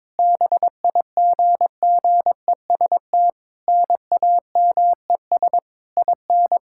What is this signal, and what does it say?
Morse code, heard as a keyed tone, about 22 wpm, 700 Hz. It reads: BIGGEST NAMES IN
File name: practice / 22wpm / Sets of 7 Words - 22wpm